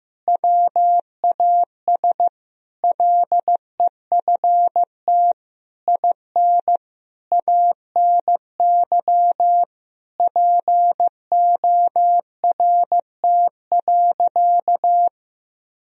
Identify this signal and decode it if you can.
Morse code, heard as a keyed tone, about 15 wpm, 700 Hz. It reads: WAS LEFT IN ANY PORT.
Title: Sentences from Top 1000 Words - Review Part 2 - 15wpm